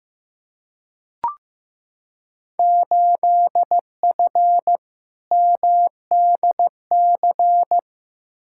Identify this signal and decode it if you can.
Morse code, heard as a keyed tone, about 15 wpm, 700 Hz. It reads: 8F MDC
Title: Field Day Exchanges Encore - 15wpm